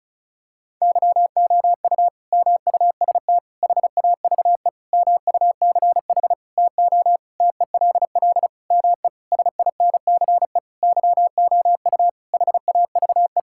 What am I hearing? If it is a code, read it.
Morse code, heard as a keyed tone, about 36 wpm, 700 Hz. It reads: YOU MUST HAVE MUCH TO TELL ME SINCE YOU HAVE
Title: Sentences from Top 1600 Words 35wpm